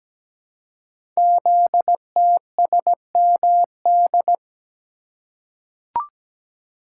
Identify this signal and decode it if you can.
Morse code, heard as a keyed tone, about 17 wpm, 700 Hz. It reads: ZTSMD
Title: Code Groups - 5 Letters - 17wpm